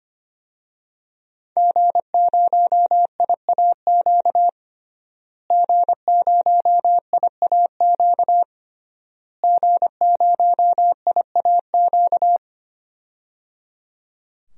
Morse code, heard as a keyed tone, about 25 wpm, 700 Hz. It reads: G0IAQ G0IAQ G0IAQ
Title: DXCC-100 Call Signs - 5 Characters - Repeated 3x - 25wpm